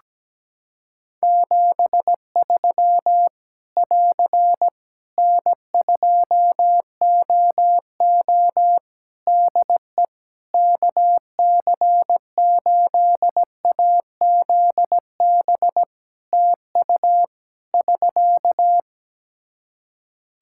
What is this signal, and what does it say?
Morse code, heard as a keyed tone, about 17 wpm, 700 Hz. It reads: 73 <AR> N2OO DE KC8AZB TU <SK>
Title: Practice QSO - Exchange 4 - Ending - 17wpm